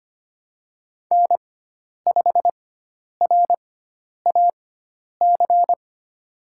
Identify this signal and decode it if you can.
Morse code, heard as a keyed tone, about 25 wpm, spaced as for 10 wpm, 700 Hz. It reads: N5RAC
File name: Lesson 16g - Letter c - Call Signs 25wpm